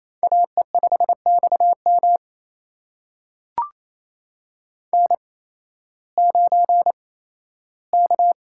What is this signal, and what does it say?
Morse code, heard as a keyed tone, about 27 wpm, spaced as for 15 wpm, 700 Hz. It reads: AE5XM N9K
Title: USA Call Signs - 5 Characters - 8x Character Spacing - 28wpm